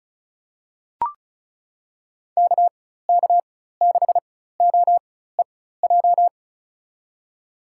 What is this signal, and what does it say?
Morse code, heard as a keyed tone, about 34 wpm, spaced as for 16 wpm, 700 Hz. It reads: KK6OEJ
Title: USA Call Signs - 6 Characters - 4x Character Spacing - 35wpm